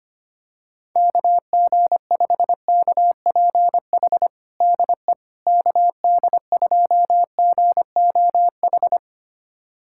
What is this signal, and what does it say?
Morse code, heard as a keyed tone, about 25 wpm, 700 Hz. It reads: KG5KPH DE KD2GOH
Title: USA Callsign DE Callsign 25wpm